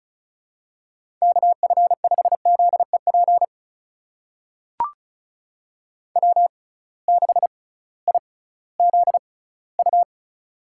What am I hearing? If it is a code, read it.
Morse code, heard as a keyed tone, about 34 wpm, spaced as for 19 wpm, 700 Hz. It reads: KF5ZEP W6IZU
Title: USA Call Signs - 6x Character Spacing - 35wpm